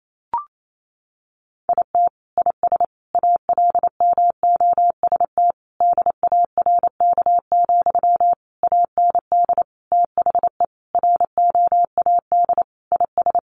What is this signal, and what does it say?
Morse code, heard as a keyed tone, about 28 wpm, 700 Hz. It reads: IT IS ALMOST DARK, AND THE ROAD IS
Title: Sentences from Top 1200 Words 28wpm